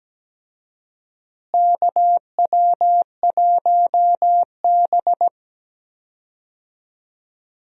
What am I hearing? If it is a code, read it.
Morse code, heard as a keyed tone, about 17 wpm, 700 Hz. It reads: KW1B
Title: USA Call Signs - 4 Characters - Repeated 3x - 17wpm